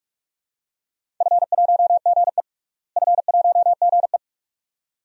Morse code, heard as a keyed tone, about 44 wpm, 700 Hz. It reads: F1GE F1GE
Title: DXCC-100 Call Signs - 4 Characters - Repeated 2x - 45wpm